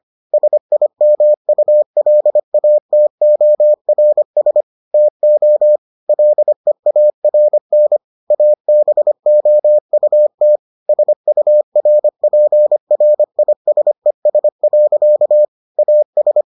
Morse code, heard as a keyed tone, about 25 wpm, 600 Hz. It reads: SIMULATORS TO LEARN ABOUT SURPRISES. AS